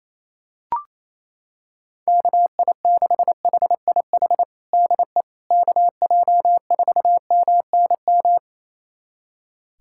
Morse code, heard as a keyed tone, about 28 wpm, 700 Hz. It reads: KI6HIH DE KJ4MNM
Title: USA Callsign DE Callsign 28wpm